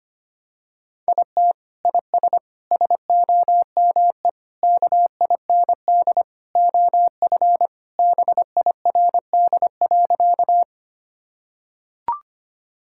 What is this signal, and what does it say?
Morse code, heard as a keyed tone, about 25 wpm, 700 Hz. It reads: IT IS SOME KIND OF BIRD.
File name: practice / 25wpm / Sentences with 3-Word Phrases - 25wpm